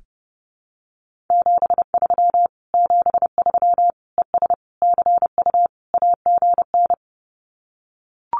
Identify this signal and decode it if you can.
Morse code, heard as a keyed tone, about 30 wpm, 700 Hz. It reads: 73 73 ES CU AGN
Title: Common QSO Phrases Encore 30wpm